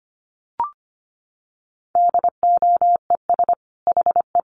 Morse code, heard as a keyed tone, about 25 wpm, 700 Hz. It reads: DOES HE
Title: Sentences from Top 1100 Words 25wpm